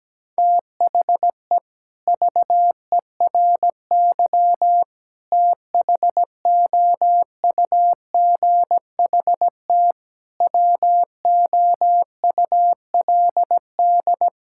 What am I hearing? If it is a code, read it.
Morse code, heard as a keyed tone, about 17 wpm, 700 Hz. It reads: THE VERY THOUGHT WOULD
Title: Sentences from Top 1000 Words - Review Part 5 - 17wpm